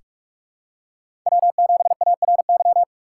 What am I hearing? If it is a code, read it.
Morse code, heard as a keyed tone, about 47 wpm, 690 Hz. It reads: W7ARY
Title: USA Call Signs - 5 Characters - 2x Character Spacing - 45wpm